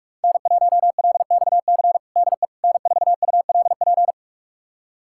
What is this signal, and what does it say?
Morse code, heard as a keyed tone, about 46 wpm, 700 Hz. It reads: N1LXC DE N4ULP
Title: USA Callsign DE Callsign - 5 Characters - 45wpm